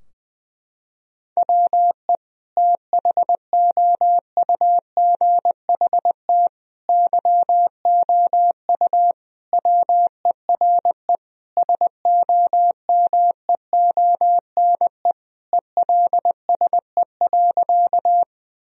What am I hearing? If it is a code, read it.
Morse code, heard as a keyed tone, about 20 wpm, 700 Hz. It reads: WE THOUGHT YOU WERE SOMEONE ELSE.
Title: Sentences from Top 1000 Words - Review Part 4 - 20wpm